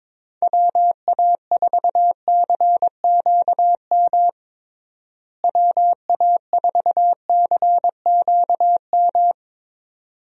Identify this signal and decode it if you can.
Morse code, heard as a keyed tone, about 22 wpm, 700 Hz. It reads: WA4CQM WA4CQM
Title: USA Call Signs - 6 Characters - Repeated 2x - 22wpm